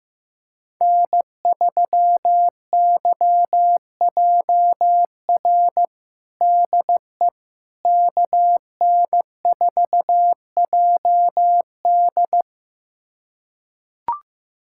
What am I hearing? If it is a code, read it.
Morse code, heard as a keyed tone, about 15 wpm, 700 Hz. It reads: N3YJR DE KN4JD
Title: USA Callsign DE Callsign - 5 Characters - 15wpm